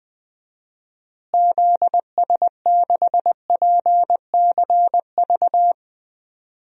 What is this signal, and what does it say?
Morse code, heard as a keyed tone, about 20 wpm, 700 Hz. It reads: ZS6PCV
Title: DXCC-100 Call Signs - 6 Characters - Repeated 2x - 20wpm